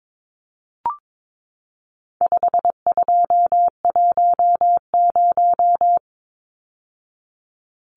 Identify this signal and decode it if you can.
Morse code, heard as a keyed tone, about 22 wpm, 700 Hz. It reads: 5210
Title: Numbers - 4-Digit 22wpm